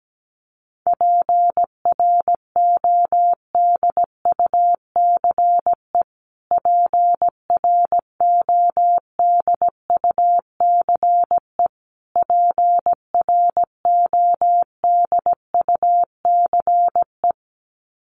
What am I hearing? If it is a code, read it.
Morse code, heard as a keyed tone, about 17 wpm, 700 Hz. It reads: PRODUCE PRODUCE PRODUCE
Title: Top 500 Words Repeated 17wpm